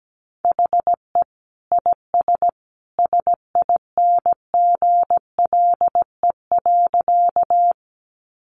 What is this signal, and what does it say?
Morse code, heard as a keyed tone, about 17 wpm, 700 Hz. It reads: HE IS SINGLE.
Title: Sentences from Top 700 Words 17wpm